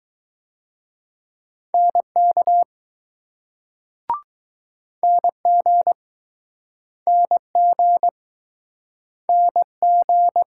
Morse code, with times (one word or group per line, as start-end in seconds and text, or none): 1.7-2.6 NK
5.0-5.9 NG
7.1-8.1 NG
9.3-10.4 NG